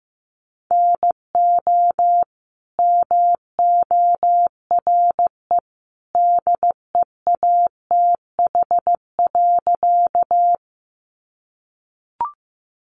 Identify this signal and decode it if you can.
Morse code, heard as a keyed tone, about 15 wpm, 700 Hz. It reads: NO MORE DEATH.
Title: Sentences from Top 1100 Words 15wpm